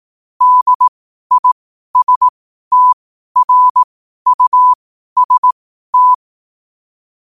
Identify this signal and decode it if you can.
Morse code, heard as a keyed tone, about 18 wpm, spaced as for 12 wpm, 1000 Hz. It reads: DISTRUST
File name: qotdfiles / 12wpm